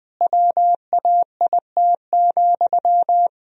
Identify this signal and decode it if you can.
Morse code, heard as a keyed tone, about 21 wpm, 700 Hz. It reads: WAIT,
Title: Sentences from Top 1000 Words - Review Part 1 - 20wpm